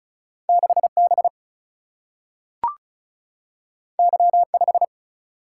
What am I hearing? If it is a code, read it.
Morse code, heard as a keyed tone, about 36 wpm, 700 Hz. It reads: 6B Y5
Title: Single-Letter Single-Number 35wpm